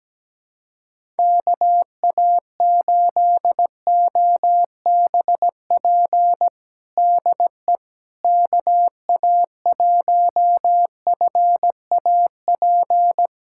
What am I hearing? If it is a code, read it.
Morse code, heard as a keyed tone, about 17 wpm, 700 Hz. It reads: KA8OBP DE KA1FAP
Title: USA Callsign DE Callsign - 6 Characters - 17wpm